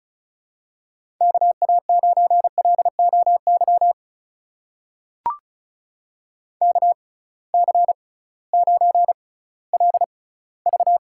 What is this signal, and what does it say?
Morse code, heard as a keyed tone, about 34 wpm, spaced as for 19 wpm, 700 Hz. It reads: KA9LOY KC9LV
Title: USA Call Signs - 6 Characters - 6x Character Spacing - 35wpm